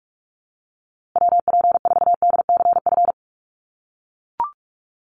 Morse code, heard as a keyed tone, about 44 wpm, 700 Hz. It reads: WP4DCF
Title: USA Call Signs Repeated 2x 45wpm